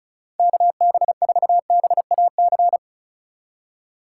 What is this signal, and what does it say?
Morse code, heard as a keyed tone, about 34 wpm, 700 Hz. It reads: KB4BAC
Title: USA Call Signs - 6 Characters - 4x Character Spacing - 35wpm